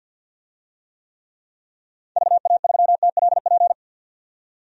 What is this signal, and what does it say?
Morse code, heard as a keyed tone, about 50 wpm, 700 Hz. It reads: VA3TLP
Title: DXCC-100 Call Signs - 6 Characters - 50wpm